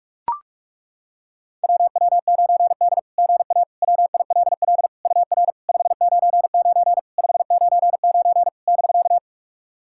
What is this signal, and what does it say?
Morse code, heard as a keyed tone, about 42 wpm, 690 Hz. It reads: WW9D GA WILL UR 599 599 <BK>
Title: SOTA QSO Practice 45wpm